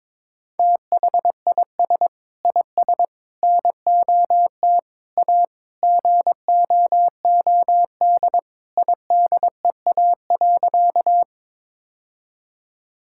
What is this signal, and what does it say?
Morse code, heard as a keyed tone, about 22 wpm, 700 Hz. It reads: THIS IS NOT A GOOD IDEA.
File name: practice / 22wpm / Sentences with 4-Word Phrases - 22wpm